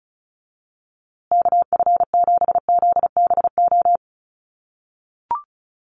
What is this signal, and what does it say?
Morse code, heard as a keyed tone, about 34 wpm, 700 Hz. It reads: KF7ZBO
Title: USA Call Signs Repeated 3x 35wpm